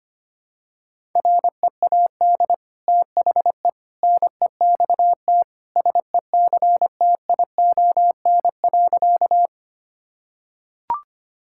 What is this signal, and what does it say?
Morse code, heard as a keyed tone, about 25 wpm, 700 Hz. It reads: READ THE NEXT SECTION.
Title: Sentences from Top 1000 Words - Review Part 5 - 25wpm